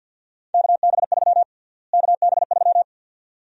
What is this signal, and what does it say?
Morse code, heard as a keyed tone, about 48 wpm, 700 Hz. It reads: KB3 KB3
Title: USA Call Sign Prefixes - Repeated 2x - 50wpm